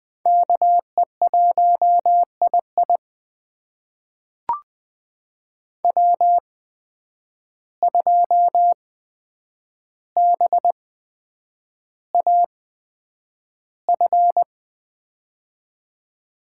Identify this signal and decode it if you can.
Morse code, heard as a keyed tone, about 20 wpm, spaced as for 9 wpm, 700 Hz. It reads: KE1II W2BAF
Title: USA Call Signs - 5 Characters - 8x Character Spacing - 20wpm